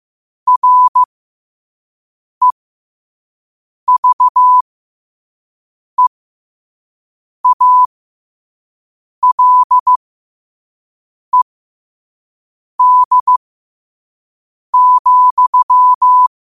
Morse code, heard as a keyed tone, about 15 wpm, spaced as for 5 wpm, 1000 Hz. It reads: REVEALED,
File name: qotdfiles / 5wpm